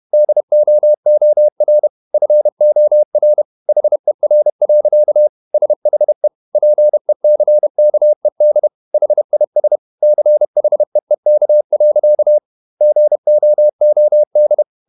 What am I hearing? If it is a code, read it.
Morse code, heard as a keyed tone, about 31 wpm, 600 Hz. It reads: DOOR FOR HER. SHE PECKED HIS CHEEK. GOOD
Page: Never 31wpm | CW med Gnister